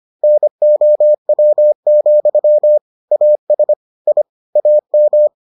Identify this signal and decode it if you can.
Morse code, heard as a keyed tone, about 25 wpm, 600 Hz. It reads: NOW, AS I AM